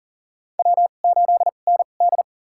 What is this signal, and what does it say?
Morse code, heard as a keyed tone, about 40 wpm, spaced as for 29 wpm, 700 Hz. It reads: W8ND
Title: USA Call Signs - 4 Characters - 2x Character Spacing - 40wpm